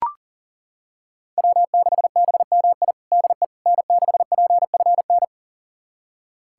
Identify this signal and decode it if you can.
Morse code, heard as a keyed tone, about 40 wpm, 700 Hz. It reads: W6BMI DE N6PFN
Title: USA Callsign DE Callsign - 5 Characters - 40wpm